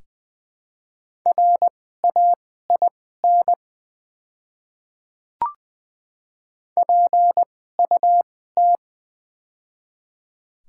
Morse code, heard as a keyed tone, about 20 wpm, spaced as for 14 wpm, 700 Hz. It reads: RAIN PUT
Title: 2x Character Spacing - Top 300 Words - 20wpm